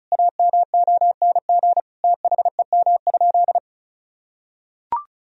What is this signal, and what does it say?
Morse code, heard as a keyed tone, about 34 wpm, 700 Hz. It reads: AMONG THEM?
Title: Sentences from Top 1000 Words - Review Part 2 - 35wpm